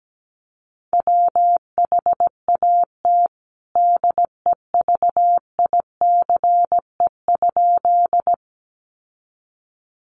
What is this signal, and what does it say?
Morse code, heard as a keyed tone, about 17 wpm, 700 Hz. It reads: WHAT DEVICE?
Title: Sentences from Top 1500 Words 17wpm